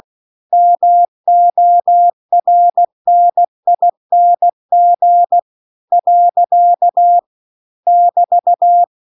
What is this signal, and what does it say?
Morse code, heard as a keyed tone, about 16 wpm, 700 Hz. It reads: MORNING . <BT>
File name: CW-The-Big-Twoheaded-River-2-16WPM